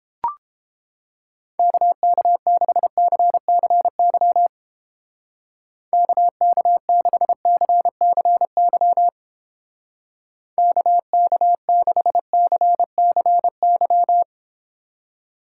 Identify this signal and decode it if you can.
Morse code, heard as a keyed tone, about 29 wpm, 700 Hz. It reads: KK6CCY KK6CCY KK6CCY